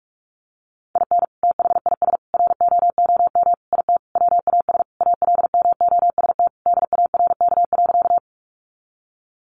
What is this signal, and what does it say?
Morse code, heard as a keyed tone, about 44 wpm, 700 Hz. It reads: IN THIS ROOM IT WAS ALMOST DARK.
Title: Sentences from Top 700 Words 45wpm